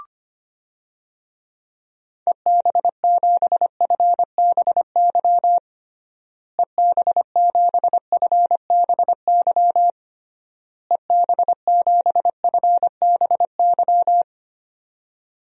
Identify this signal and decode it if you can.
Morse code, heard as a keyed tone, about 25 wpm, 700 Hz. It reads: EB7FBY EB7FBY EB7FBY
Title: DXCC-100 Call Signs - 6 Characters - Repeated 3x - 25wpm